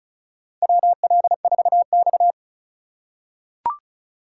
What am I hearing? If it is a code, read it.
Morse code, heard as a keyed tone, about 36 wpm, 700 Hz. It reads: WL4X
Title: USA Call Signs - 4 Characters - 2x Character Spacing - 35wpm